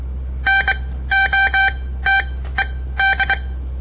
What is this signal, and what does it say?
Morse code, heard as a keyed tone, about 22 wpm, spaced as for 15 wpm, 1550 Hz. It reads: NOTED
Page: On the air Morse code practice from the Southern Montana Amatuer Radio Association; all code, no read back